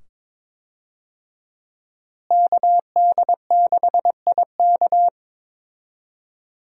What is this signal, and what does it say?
Morse code, heard as a keyed tone, about 22 wpm, 700 Hz. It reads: KD6IK
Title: USA Call Signs - 5 Characters - Repeated 2x - 22wpm